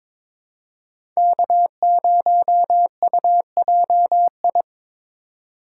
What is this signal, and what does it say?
Morse code, heard as a keyed tone, about 22 wpm, 700 Hz. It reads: K0UJI
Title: USA Call Signs - 5 Characters - 4x Character Spacing - 22wpm